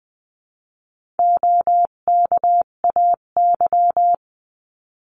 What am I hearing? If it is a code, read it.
Morse code, heard as a keyed tone, about 20 wpm, 700 Hz. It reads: OKAY
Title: 258 Lesson 42c - Words 16-20wpm